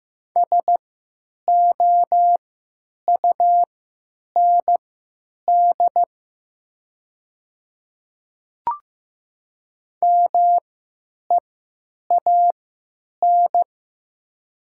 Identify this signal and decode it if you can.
Morse code, heard as a keyed tone, about 15 wpm, spaced as for 9 wpm, 700 Hz. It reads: SOUND MEAN
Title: 3x Character Spacing - Top 300 Words - 15wpm